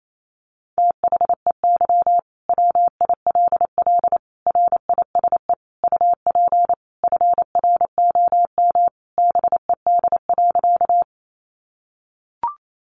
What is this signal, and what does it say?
Morse code, heard as a keyed tone, about 28 wpm, 700 Hz. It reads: THEY WILL RISE UP FROM BED.
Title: Sentences from Top 900 Words 28wpm